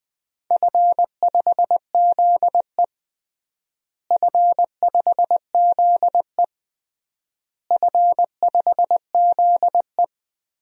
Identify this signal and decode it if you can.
Morse code, heard as a keyed tone, about 20 wpm, 700 Hz. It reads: F5ZE F5ZE F5ZE
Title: DXCC-100 Call Signs - 4 Characters - Repeated 3x - 20wpm